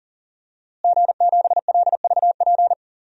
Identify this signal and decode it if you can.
Morse code, heard as a keyed tone, about 40 wpm, 700 Hz. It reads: G7LVP
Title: DXCC-100 Call Signs - 5 Characters - Repeated 2x - 40wpm